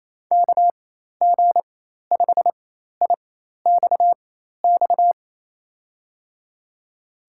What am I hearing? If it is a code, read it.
Morse code, heard as a keyed tone, about 28 wpm, spaced as for 13 wpm, 700 Hz. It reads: KG5IXX
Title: USA Call Signs - 6 Characters - 4x Character Spacing - 28wpm